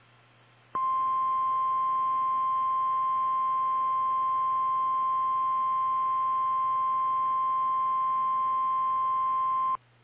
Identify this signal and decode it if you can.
Emergency Alert System (SAME) tones (the 1050 Hz attention tone). It is heard from 0.7 to 9.8 s.
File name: eas / KKPRKXPN / KKPRKXPN